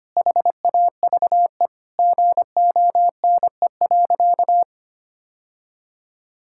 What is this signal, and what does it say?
Morse code, heard as a keyed tone, about 25 wpm, 700 Hz. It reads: HAVE GONE.
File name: practice / 25wpm / Sentences with 3-Word Phrases - 25wpm